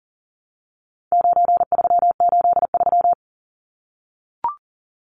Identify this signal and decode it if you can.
Morse code, heard as a keyed tone, about 40 wpm, 700 Hz. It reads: 9383
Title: Numbers - 4-Digit 40wpm